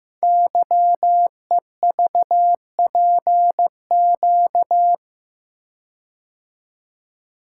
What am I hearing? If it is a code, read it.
Morse code, heard as a keyed tone, about 15 wpm, 700 Hz. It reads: YEVPQ
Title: Code Groups - 5 Letters - 15wpm